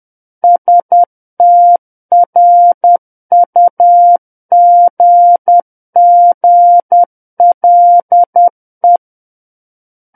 Morse code, heard as a keyed tone, about 10 wpm, 700 Hz. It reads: STRUGGLE
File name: Book One / 01-Chapter 1 - 10 WPM